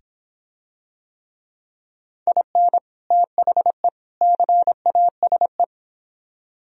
Morse code, heard as a keyed tone, about 26 wpm, 700 Hz. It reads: IN THE CASE